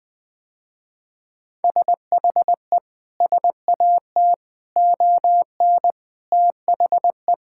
Morse code, heard as a keyed tone, about 19 wpm, 700 Hz. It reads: SHE SAT ON THE
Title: Sentences with 6-Word Phrases - Spoken Beginning - 20wpm